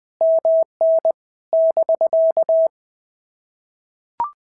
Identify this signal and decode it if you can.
Morse code, heard as a keyed tone, about 20 wpm, 650 Hz. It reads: MN <BK>
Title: POTA QSO Practice - 20wpm